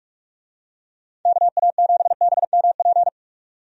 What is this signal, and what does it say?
Morse code, heard as a keyed tone, about 46 wpm, 700 Hz. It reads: KA7BMP
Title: USA Call Signs - 6 Characters - 45wpm